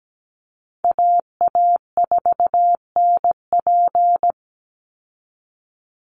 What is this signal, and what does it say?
Morse code, heard as a keyed tone, about 17 wpm, 700 Hz. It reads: AA4NP
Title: Call Signs Repeated 2x - 17wpm